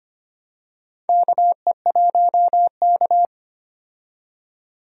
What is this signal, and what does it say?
Morse code, heard as a keyed tone, about 25 wpm, 700 Hz. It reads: KE1K
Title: USA Call Signs - 4 Characters - 25wpm